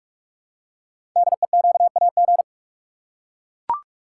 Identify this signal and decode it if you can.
Morse code, heard as a keyed tone, about 44 wpm, 690 Hz. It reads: DEQAG
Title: Code Groups - 5 Letters - 45wpm